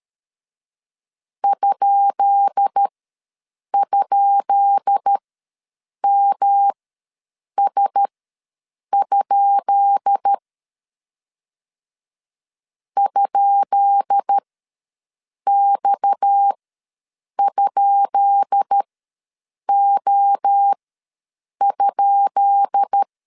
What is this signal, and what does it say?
Morse code, heard as a keyed tone, about 13 wpm, spaced as for 7 wpm, 800 Hz. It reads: ??MS? ?X?O?
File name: LEZIONI / 10 SIGNS / 10B INT ws 12wpm 3527